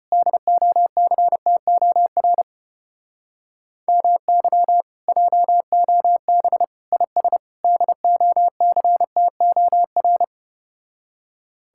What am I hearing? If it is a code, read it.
Morse code, heard as a keyed tone, about 33 wpm, 700 Hz. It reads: DOCTOR MY JOB IS DOCTOR